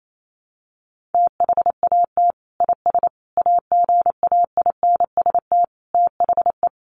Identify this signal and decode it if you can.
Morse code, heard as a keyed tone, about 28 wpm, 700 Hz. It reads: THAT IS AGAINST THE
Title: Sentences from Top 1500 Words 28wpm